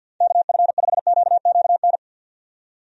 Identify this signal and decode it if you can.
Morse code, heard as a keyed tone, about 51 wpm, 700 Hz. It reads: KF5QQN
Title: USA Call Signs - 6 Characters - 50wpm